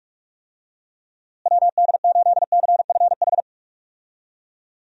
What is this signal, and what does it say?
Morse code, heard as a keyed tone, about 42 wpm, 700 Hz. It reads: WD8CFH
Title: USA Call Signs - 6 Characters - Repeated 2x - 45wpm